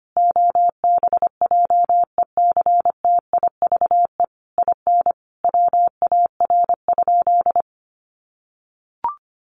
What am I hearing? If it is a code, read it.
Morse code, heard as a keyed tone, about 25 wpm, 700 Hz. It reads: OBJECTIVE IN WAR?
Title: Sentences from Top 1200 Words 25wpm